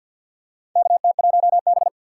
Morse code, heard as a keyed tone, about 52 wpm, 700 Hz. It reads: KT1B
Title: K1USN SST QSO Practice - 50wpm